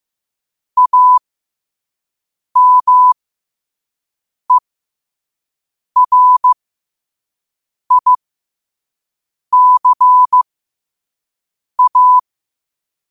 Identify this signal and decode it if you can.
Morse code, heard as a keyed tone, about 15 wpm, spaced as for 5 wpm, 1000 Hz. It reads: AMERICA
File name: qotdfiles / 5wpm